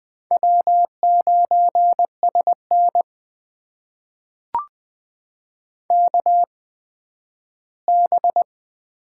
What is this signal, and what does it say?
Morse code, heard as a keyed tone, about 20 wpm, spaced as for 12 wpm, 700 Hz. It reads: W9SN KB
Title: USA Call Signs - 8x Character Spacing - 20wpm